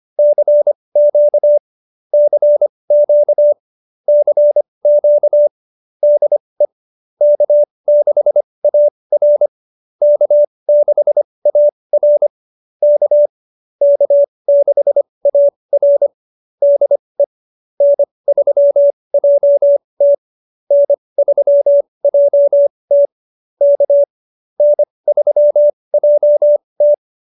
Morse code, heard as a keyed tone, about 25 wpm, spaced as for 20 wpm, 590 Hz. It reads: CQ CQ CQ DE K6AR K6AR K K6AR DE N3JT N3JT K N3JT
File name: QSO307 20